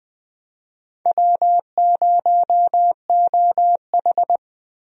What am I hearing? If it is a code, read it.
Morse code, heard as a keyed tone, about 20 wpm, 700 Hz. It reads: W0OH
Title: USA Call Signs - 4 Characters - Repeated 3x - 20wpm